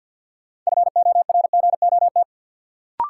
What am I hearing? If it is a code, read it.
Morse code, heard as a keyed tone, about 53 wpm, 700 Hz. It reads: FORGOT
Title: Sentences from Top 1400 Words - New Word Drill - 50wpm